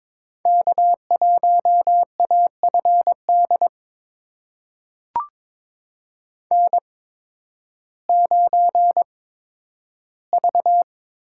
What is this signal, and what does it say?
Morse code, heard as a keyed tone, about 21 wpm, spaced as for 12 wpm, 700 Hz. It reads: K1AFD N9V
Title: USA Call Signs - 5 Characters - 8x Character Spacing - 22wpm